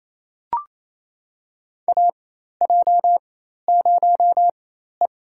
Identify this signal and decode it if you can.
Morse code, heard as a keyed tone, about 28 wpm, spaced as for 13 wpm, 700 Hz. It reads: AJ0E
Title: USA Call Signs - 4 Characters - 4x Character Spacing - 28wpm